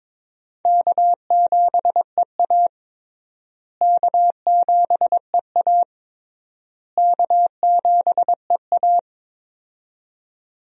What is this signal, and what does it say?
Morse code, heard as a keyed tone, about 22 wpm, 700 Hz. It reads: K7EA K7EA K7EA
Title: USA Call Signs - 4 Characters - Repeated 3x - 22wpm